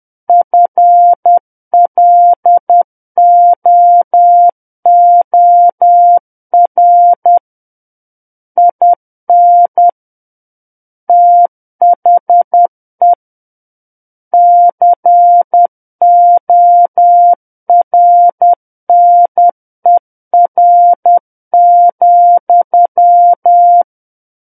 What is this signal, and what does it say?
Morse code, heard as a keyed tone, about 10 wpm, 700 Hz. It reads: FLOOR IN THE CORNER,
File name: Book One / 01-Chapter 1 - 10 WPM